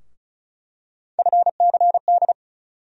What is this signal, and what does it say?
Morse code, heard as a keyed tone, about 36 wpm, 700 Hz. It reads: FCD
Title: USA Call Sign Postfixes - 3 Letters 35wpm